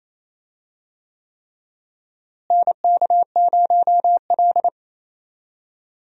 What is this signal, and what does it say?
Morse code, heard as a keyed tone, about 28 wpm, 700 Hz. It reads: NK0L
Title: USA Call Signs - 4 Characters - Repeated 3x - 28wpm